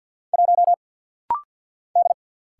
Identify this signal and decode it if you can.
Morse code, heard as a keyed tone, about 50 wpm, 700 Hz. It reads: 1 D
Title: Single Letter-Number - Rapid-Fire 50wpm